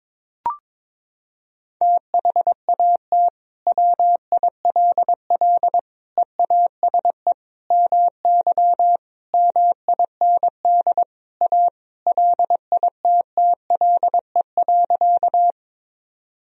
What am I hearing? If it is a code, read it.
Morse code, heard as a keyed tone, about 22 wpm, 700 Hz. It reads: THAT WILL EASE MY MIND A LITTLE.
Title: Sentences from Top 1000 Words - Review Part 1 - 22wpm